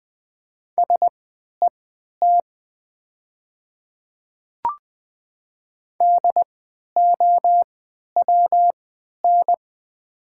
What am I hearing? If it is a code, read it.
Morse code, heard as a keyed tone, about 20 wpm, spaced as for 11 wpm, 700 Hz. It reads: SET DOWN
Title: 3x Character Spacing - Top 200 Words - 20wpm